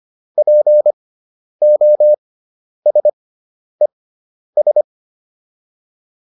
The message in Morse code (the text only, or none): POSES